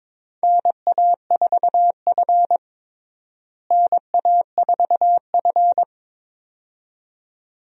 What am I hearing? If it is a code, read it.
Morse code, heard as a keyed tone, about 22 wpm, 700 Hz. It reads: NA4F NA4F
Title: USA Call Signs - 4 Characters - Repeated 2x - 22wpm